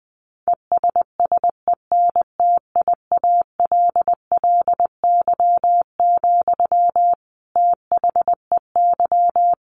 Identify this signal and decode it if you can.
Morse code, heard as a keyed tone, about 20 wpm, 700 Hz. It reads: ESSENTIALLY, THEY
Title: Sentences from Top 1300 Words 20wpm